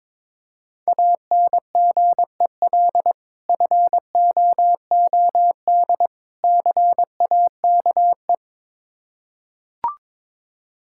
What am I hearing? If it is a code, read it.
Morse code, heard as a keyed tone, about 22 wpm, 700 Hz. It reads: ANGEL FOOD CAKE
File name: Difficult Words with Hints - 22wpm